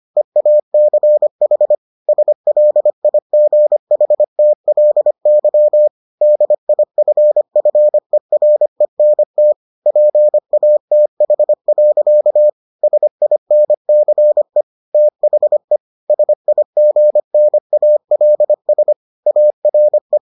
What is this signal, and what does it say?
Morse code, heard as a keyed tone, about 25 wpm, 600 Hz. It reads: EACH SLIGHTLY DIFFERENT PATH. SINCE THE SIGNALS ARE